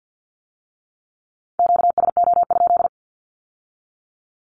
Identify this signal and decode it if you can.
Morse code, heard as a keyed tone, about 50 wpm, 700 Hz. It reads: QSO?
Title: Q-Codes with Spoken Meaning - 50wpm